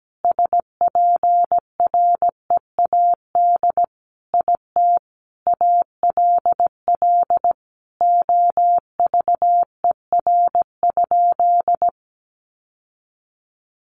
Morse code, heard as a keyed tone, about 17 wpm, 700 Hz. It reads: SPREAD IT ALL OVER?
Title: Sentences from Top 1200 Words 17wpm